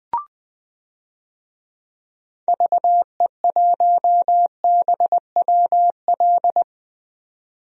Morse code, heard as a keyed tone, about 20 wpm, 700 Hz. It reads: VE1BWL
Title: DXCC-100 Call Signs - 6 Characters - 20wpm